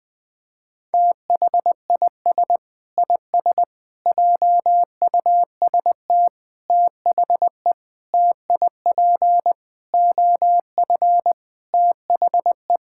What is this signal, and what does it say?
Morse code, heard as a keyed tone, about 20 wpm, 700 Hz. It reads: THIS IS JUST THE TIP OF THE
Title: Sets of 8 Words - 20wpm